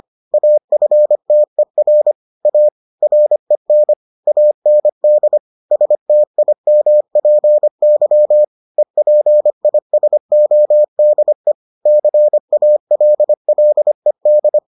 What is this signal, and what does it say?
Morse code, heard as a keyed tone, about 25 wpm, 600 Hz. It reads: AFTER A REN AND STIMPY EPISODE CALLED